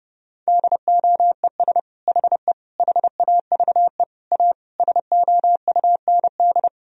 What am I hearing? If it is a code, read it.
Morse code, heard as a keyed tone, about 30 wpm, 700 Hz. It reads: DOES HE HAVE A SOUND
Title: Sentences from Top 1000 Words - Review Part 2 - 30wpm